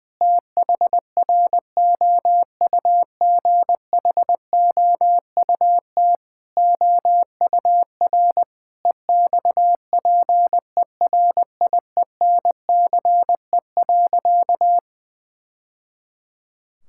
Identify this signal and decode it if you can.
Morse code, heard as a keyed tone, about 21 wpm, 700 Hz. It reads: THROUGHOUT OUR EXPERIENCE.
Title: Sentences from Top 1000 Words - Review Part 5 - 20wpm